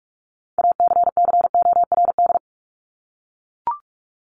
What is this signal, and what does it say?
Morse code, heard as a keyed tone, about 44 wpm, 700 Hz. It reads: ACCORD
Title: ING Suffix - Top 100 Words with Base Words 45wpm